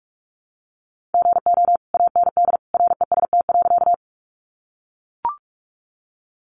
Morse code, heard as a keyed tone, about 44 wpm, 700 Hz. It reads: GO AND REST.
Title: Sentences from Top 500 Words 45wpm